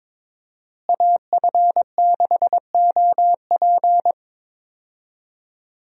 Morse code, heard as a keyed tone, about 22 wpm, 700 Hz. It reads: AF6OP
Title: USA Call Signs - 5 Characters - Repeated 2x - 22wpm